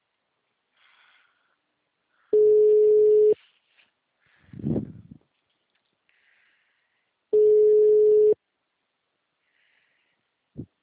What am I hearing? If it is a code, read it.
Ringback tone (Europe, 425 Hz).
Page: Nova Iguacu - RJ